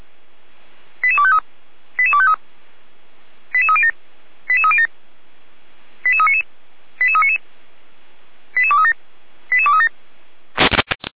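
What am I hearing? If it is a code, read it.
Five-tone selective call (ZVEI), heard as quick runs of five tones, tones of about 70 ms. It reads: 88253 88253 88307 88307 88399 88399 88237 88237
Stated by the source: ZVEI five-tone selective call received on an SDR.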